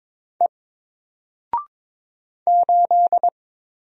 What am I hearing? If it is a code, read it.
Morse code, heard as a keyed tone, about 22 wpm, 700 Hz. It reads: E 8
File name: Single-Letter-Number-22wpm